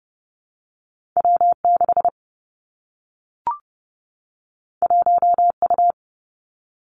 Morse code, heard as a keyed tone, about 30 wpm, 700 Hz. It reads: W 6 1 U
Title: Two Challenging Letter-Number Combinations 30wpm